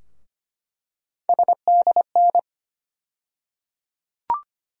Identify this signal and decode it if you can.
Morse code, heard as a keyed tone, about 25 wpm, 700 Hz. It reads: SDN
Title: USA Call Sign Postfixes - 3 Letters 25wpm